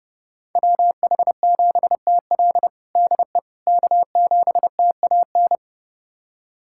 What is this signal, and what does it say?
Morse code, heard as a keyed tone, about 30 wpm, 700 Hz. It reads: WH7TL DE K7TAN
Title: USA Callsign DE Callsign - 5 Characters - 30wpm